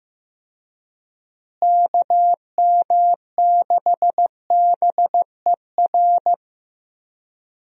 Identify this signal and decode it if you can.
Morse code, heard as a keyed tone, about 16 wpm, 700 Hz. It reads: KM6BER
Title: USA Call Signs - 6 Characters - Repeated 3x - 15wpm